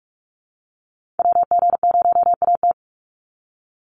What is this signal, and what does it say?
Morse code, heard as a keyed tone, about 44 wpm, 700 Hz. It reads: WG0AT
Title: NA SOTA Activator Callsigns 45wpm